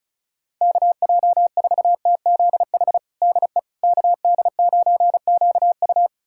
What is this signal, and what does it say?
Morse code, heard as a keyed tone, about 34 wpm, 700 Hz. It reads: KJ4TZH DE KD9QU
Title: USA Callsign DE Callsign 35wpm